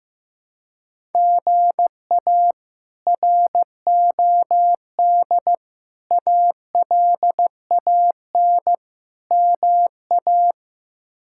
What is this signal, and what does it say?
Morse code, heard as a keyed tone, about 16 wpm, 700 Hz. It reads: GA ROD ALAN MA
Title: K1USN SST QSO Practice - 15wpm